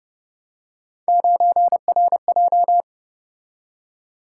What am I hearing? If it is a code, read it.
Morse code, heard as a keyed tone, about 30 wpm, 700 Hz. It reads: 9RJ
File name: Speed-Racing - Three Letter-Number Combinations - 25wpm